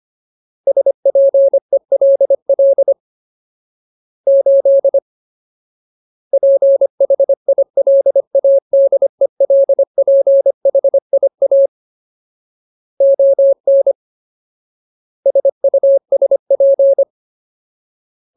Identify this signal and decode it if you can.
Morse code, heard as a keyed tone, about 25 wpm, 550 Hz. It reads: SPELL 8 PHILADELPHIA ON SUSP
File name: Ultimate Word List / 25WPM Ultimate